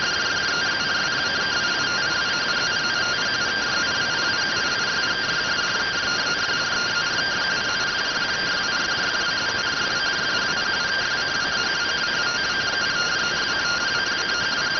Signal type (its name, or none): two-tone teleprinter signal